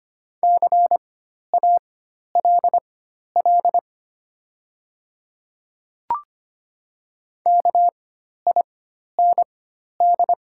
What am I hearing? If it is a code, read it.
Morse code, heard as a keyed tone, about 25 wpm, spaced as for 12 wpm, 690 Hz. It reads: CALL KIND
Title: 4x Character Spacing - Top 500 Words - 25wpm